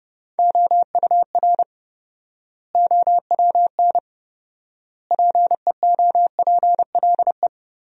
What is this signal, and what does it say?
Morse code, heard as a keyed tone, about 32 wpm, 700 Hz. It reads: OUR OWN PEOPLE
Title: 4x Word-Spacing - Sentences from Top 300 Words - 30wpm